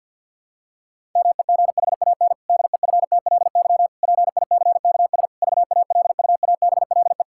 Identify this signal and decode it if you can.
Morse code, heard as a keyed tone, about 51 wpm, 700 Hz. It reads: MEGHAN DEFTLY PICKS VALUABLE
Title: Pangram Challenge v2 - 50wpm